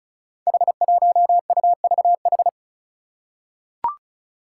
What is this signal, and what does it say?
Morse code, heard as a keyed tone, about 36 wpm, 700 Hz. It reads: H1UVH
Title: Five Challenging Letter-Number Combinations 35wpm